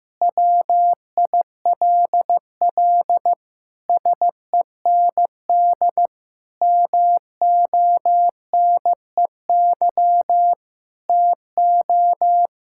Morse code, heard as a keyed tone, about 15 wpm, 700 Hz. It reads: WILL SEND MONEY TO
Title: Sentences from Top 1600 Words 15wpm